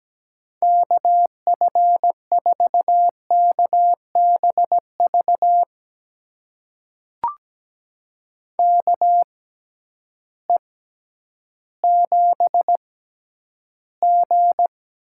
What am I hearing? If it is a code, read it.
Morse code, heard as a keyed tone, about 17 wpm, spaced as for 10 wpm, 700 Hz. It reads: KF4KBV KE7G
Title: USA Call Signs - 6x Character Spacing - 17wpm